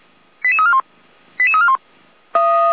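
Five-tone selective call (ZVEI), heard as quick runs of five tones, tones of about 80 ms. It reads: 88341 88341
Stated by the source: ZVEI five-tone call received on SDR.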